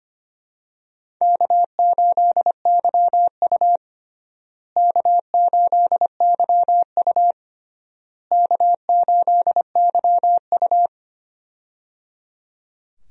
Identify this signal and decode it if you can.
Morse code, heard as a keyed tone, about 25 wpm, 700 Hz. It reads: K8YU K8YU K8YU
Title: USA Call Signs - 4 Characters - Repeated 3x - 25wpm